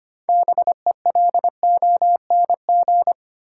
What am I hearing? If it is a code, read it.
Morse code, heard as a keyed tone, about 25 wpm, 700 Hz. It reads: BELONG
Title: Six Letter Words - 25wpm